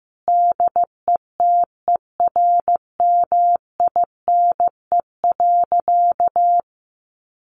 Morse code, heard as a keyed tone, about 15 wpm, 700 Hz. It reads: DETERMINE.
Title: Sentences from Top 1500 Words 15wpm